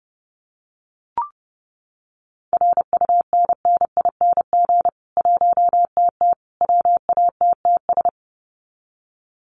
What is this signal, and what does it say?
Morse code, heard as a keyed tone, about 30 wpm, 700 Hz. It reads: RUNNING 1TT WATTS
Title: Common QSO Phrases 30wpm